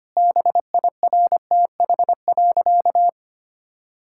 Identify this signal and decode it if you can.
Morse code, heard as a keyed tone, about 25 wpm, 700 Hz. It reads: BIRTH.
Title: Sentences from Top 1000 Words - Review Part 5 - 25wpm